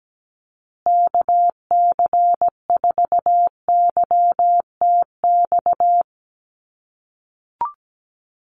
Morse code, heard as a keyed tone, about 17 wpm, 700 Hz. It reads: KC4YTX
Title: USA Call Signs Repeated 3x 17wpm